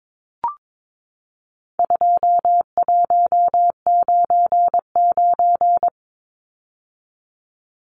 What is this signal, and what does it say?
Morse code, heard as a keyed tone, about 22 wpm, 700 Hz. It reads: 2199
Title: Numbers - 4-Digit 22wpm